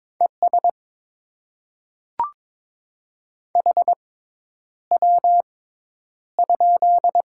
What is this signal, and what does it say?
Morse code, heard as a keyed tone, about 22 wpm, spaced as for 10 wpm, 690 Hz. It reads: ES HW?
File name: Top 20 QSO Elements - 6x Character Spacing - 22wpm